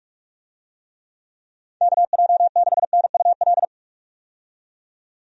Morse code, heard as a keyed tone, about 46 wpm, 700 Hz. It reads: KJ6NUL
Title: USA Call Signs - 6 Characters - 45wpm